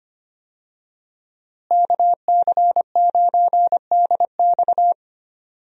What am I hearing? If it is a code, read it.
Morse code, heard as a keyed tone, about 25 wpm, 690 Hz. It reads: KC9DX
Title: USA Call Signs - 5 Characters - Repeated 2x - 25wpm